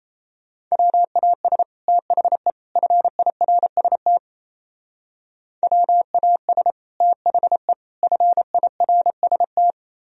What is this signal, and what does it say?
Morse code, heard as a keyed tone, about 32 wpm, 700 Hz. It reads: WAS THE FIRST WAS THE FIRST
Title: Speed-Racing - Sets of 3 Words 28wpm